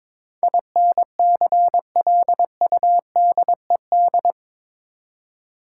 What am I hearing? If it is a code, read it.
Morse code, heard as a keyed tone, about 22 wpm, 700 Hz. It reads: INCLUDED
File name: practice / 22wpm / Eight Letter Words - 22wpm